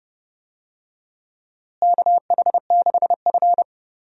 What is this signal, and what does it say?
Morse code, heard as a keyed tone, about 30 wpm, 700 Hz. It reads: KH6F
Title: USA Call Signs - 4 Characters - Repeated 2x - 30wpm